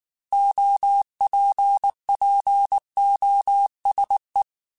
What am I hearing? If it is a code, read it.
Morse code, heard as a keyed tone, about 19 wpm, 800 Hz. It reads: OPPOSE